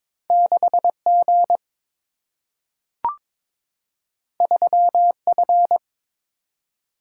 Morse code, heard as a keyed tone, about 22 wpm, 700 Hz. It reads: 6 G 3 F
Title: Single-Letter Single-Number 22wpm